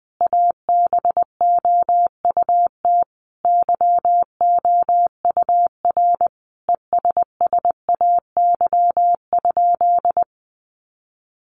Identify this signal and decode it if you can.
Morse code, heard as a keyed tone, about 20 wpm, 700 Hz. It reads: ABOUT YOUR ESSAY?
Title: Sentences from Top 1500 Words 20wpm